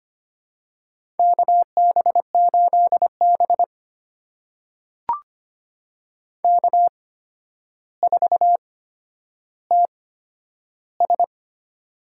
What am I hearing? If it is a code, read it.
Morse code, heard as a keyed tone, about 25 wpm, spaced as for 11 wpm, 700 Hz. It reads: KB8B K4TS
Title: USA Call Signs - 8x Character Spacing - 25wpm